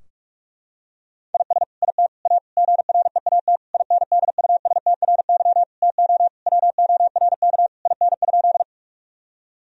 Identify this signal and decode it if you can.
Morse code, heard as a keyed tone, about 41 wpm, 700 Hz. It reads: IS IT A GREAT INDUSTRY TO WORK IN?
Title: Sentences from Top 1000 Words - Review Part 4 - 45wpm